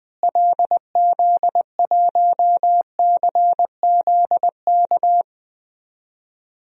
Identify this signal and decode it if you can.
Morse code, heard as a keyed tone, about 20 wpm, 700 Hz. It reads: LZ1CZK
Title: DXCC-100 Call Signs - 6 Characters - Repeated 3x - 20wpm